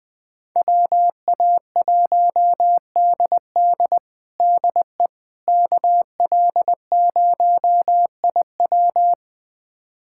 Morse code, heard as a keyed tone, about 21 wpm, 700 Hz. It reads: WA1DD DE KL0IW
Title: USA Callsign DE Callsign - 5 Characters - 20wpm